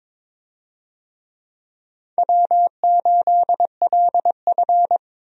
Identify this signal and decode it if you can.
Morse code, heard as a keyed tone, about 22 wpm, 700 Hz. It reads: W8LF
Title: USA Call Signs - 4 Characters - 22wpm